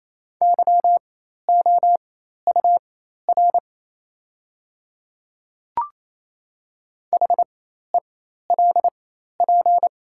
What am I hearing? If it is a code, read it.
Morse code, heard as a keyed tone, about 28 wpm, spaced as for 13 wpm, 700 Hz. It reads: YOUR HELP
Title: 4x Character Spacing - Top 500 Words - 28wpm